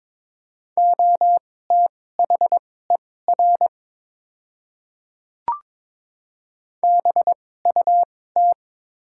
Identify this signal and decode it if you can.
Morse code, heard as a keyed tone, about 22 wpm, spaced as for 16 wpm, 700 Hz. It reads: OTHER BUT
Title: 2x Character Spacing - Top 100 Words - 22wpm